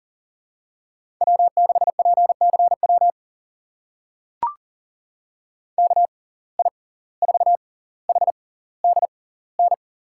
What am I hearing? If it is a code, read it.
Morse code, heard as a keyed tone, about 40 wpm, spaced as for 19 wpm, 700 Hz. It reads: W6PCW KI4HDN
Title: USA Call Signs - 6x Character Spacing - 40wpm